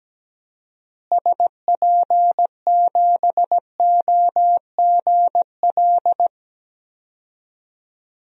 Morse code, heard as a keyed tone, about 17 wpm, 700 Hz. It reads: SP7OGL
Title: DXCC-100 Call Signs - 6 Characters - Repeated 2x - 17wpm